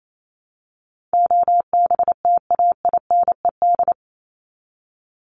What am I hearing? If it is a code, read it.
Morse code, heard as a keyed tone, about 28 wpm, 700 Hz. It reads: OBTAINED
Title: ED Suffix - Top 100 Words with Base Words 28wpm